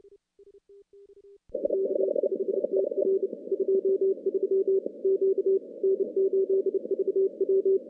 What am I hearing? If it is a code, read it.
Morse code, heard as a keyed tone, about 29 wpm, 390 Hz. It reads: K5TR 23 Q N8VW
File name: contests / CW-SS-k5tr- -2300